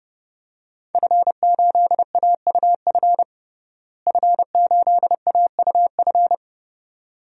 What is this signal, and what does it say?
Morse code, heard as a keyed tone, about 30 wpm, 700 Hz. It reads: F8AUF F8AUF
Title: DXCC-100 Call Signs - 5 Characters - Repeated 2x - 30wpm